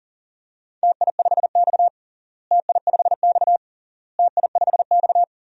Morse code, heard as a keyed tone, about 40 wpm, 700 Hz. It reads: TI5X TI5X TI5X
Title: DXCC-100 Call Signs - 4 Characters - Repeated 3x - 40wpm